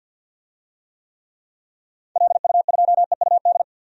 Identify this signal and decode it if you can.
Morse code, heard as a keyed tone, about 49 wpm, 700 Hz. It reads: LU2EUD
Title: DXCC-100 Call Signs - 6 Characters - 50wpm